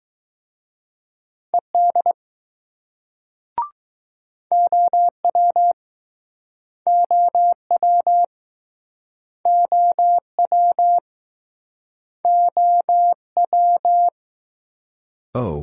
Morse code, heard as a keyed tone, about 19 wpm, 700 Hz. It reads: ED OW OW OW OW